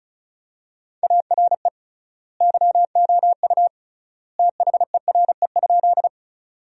Morse code, heard as a keyed tone, about 34 wpm, 700 Hz. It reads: ARE YOU THERE?
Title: 3x Word-Spacing - Sentences from Top 100 Words - 35wpm